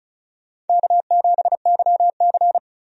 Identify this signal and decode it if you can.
Morse code, heard as a keyed tone, about 34 wpm, 700 Hz. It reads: K7YC
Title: USA Call Signs - 4 Characters - 35wpm